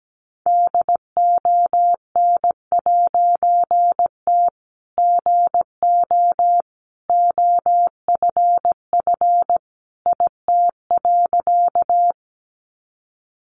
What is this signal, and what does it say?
Morse code, heard as a keyed tone, about 17 wpm, 700 Hz. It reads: DON'T GO OFF IT.
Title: A Sound Of Thunder 17wpm